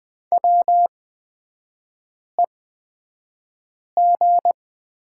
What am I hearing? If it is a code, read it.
Morse code, heard as a keyed tone, about 20 wpm, spaced as for 5 wpm, 700 Hz. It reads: WEG
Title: 243 Lesson 39e - Letter z - Three Characters 20wpm